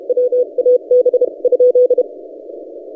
Morse code, heard as a keyed tone, about 32 wpm, 500 Hz. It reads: WAB ?
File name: S9TF 20CW